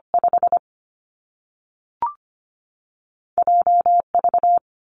Morse code, heard as a keyed tone, about 25 wpm, 700 Hz. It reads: 5 JV
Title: Two Challenging Letter-Number Combinations 25wpm